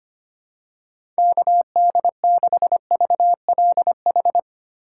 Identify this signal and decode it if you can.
Morse code, heard as a keyed tone, about 25 wpm, 700 Hz. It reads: KD6VLH
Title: USA Call Signs - 6 Characters - 4x Character Spacing - 25wpm